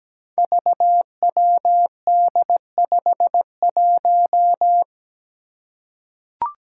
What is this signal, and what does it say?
Morse code, heard as a keyed tone, about 17 wpm, 700 Hz. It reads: VWD51
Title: Five Challenging Letter-Number Combinations 17wpm